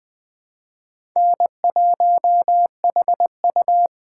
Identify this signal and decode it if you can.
Morse code, heard as a keyed tone, about 20 wpm, 700 Hz. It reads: N1HU
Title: USA Call Signs - 4 Characters - 3x Character Spacing - 20wpm